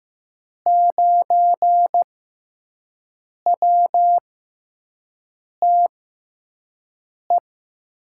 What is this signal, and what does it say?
Morse code, heard as a keyed tone, about 15 wpm, spaced as for 5 wpm, 700 Hz. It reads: 9WTE
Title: USA Call Signs - 6 Characters - 6x Character Spacing - 15wpm